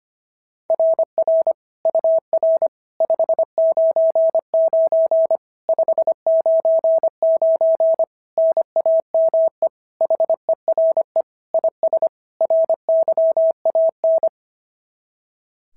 Morse code, heard as a keyed tone, about 22 wpm, 650 Hz. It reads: RR UR 599 599 NAME HERE IS RYAN
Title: SOTA QSO Practice 25wpm